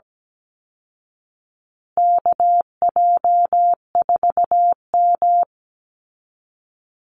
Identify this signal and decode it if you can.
Morse code, heard as a keyed tone, about 17 wpm, 700 Hz. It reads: KJ4M
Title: Call Signs Repeated 2x - 17wpm